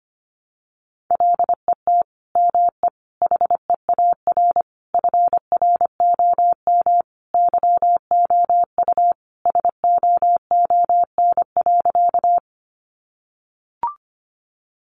Morse code, heard as a keyed tone, about 25 wpm, 700 Hz. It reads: LET ME HEAR FROM YOU SOON.
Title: Sentences from Top 400 Words 25wpm